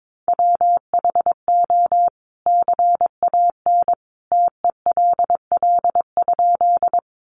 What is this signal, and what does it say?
Morse code, heard as a keyed tone, about 22 wpm, 700 Hz. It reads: WHO CAN TELL?
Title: A Sound Of Thunder 22wpm